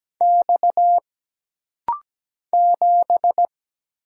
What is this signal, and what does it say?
Morse code, heard as a keyed tone, about 17 wpm, 700 Hz. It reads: X 7
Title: Single Letter-Number Alt - Rapid-Fire 17wpm